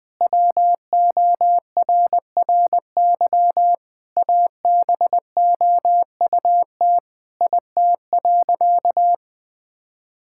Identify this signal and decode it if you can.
Morse code, heard as a keyed tone, about 21 wpm, 700 Hz. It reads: WORRY ABOUT IT.
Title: Sentences with 6-Word Phrases - 20wpm